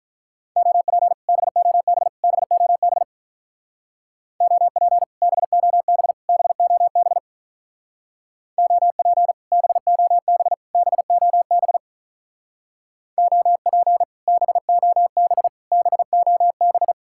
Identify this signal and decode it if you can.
Morse code, heard as a keyed tone, about 48 wpm, 700 Hz. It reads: OP BOB BOB OP BOB BOB OP BOB BOB OP BOB BOB